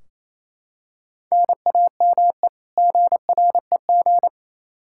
Morse code, heard as a keyed tone, about 28 wpm, 700 Hz. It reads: NAME GREG
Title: Practice QSO - Exchange 1 - Beginning - 28wpm